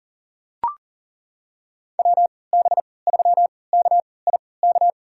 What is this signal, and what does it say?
Morse code, heard as a keyed tone, about 40 wpm, spaced as for 23 wpm, 700 Hz. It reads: WB3KIK
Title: USA Call Signs - 6 Characters - 3x Character Spacing - 40wpm